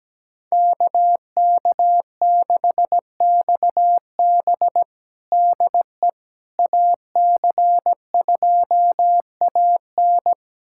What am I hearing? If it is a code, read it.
Morse code, heard as a keyed tone, about 17 wpm, 700 Hz. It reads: KK6XB DE AC2AN
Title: USA Callsign DE Callsign - 5 Characters - 17wpm